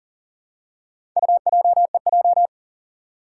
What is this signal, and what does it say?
Morse code, heard as a keyed tone, about 43 wpm, 700 Hz. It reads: UJEJ
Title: Four Challenging Letter-Number Combinations 40wpm